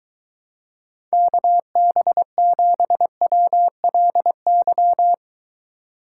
Morse code, heard as a keyed tone, about 22 wpm, 700 Hz. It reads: KB7WLY